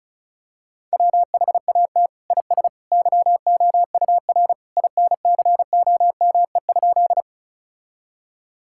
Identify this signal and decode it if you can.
Morse code, heard as a keyed tone, about 34 wpm, 700 Hz. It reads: WHAT IS YOUR INCOME?
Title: Sentences from Top 1000 Words - Review Part 5 - 35wpm